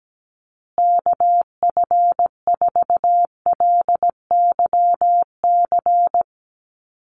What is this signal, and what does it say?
Morse code, heard as a keyed tone, about 17 wpm, 700 Hz. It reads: KF4LYC
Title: USA Call Signs Repeated 2x 17wpm